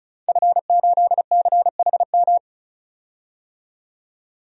Morse code, heard as a keyed tone, about 34 wpm, 700 Hz. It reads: F8CHM
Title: DXCC-100 Call Signs - 5 Characters - 35wpm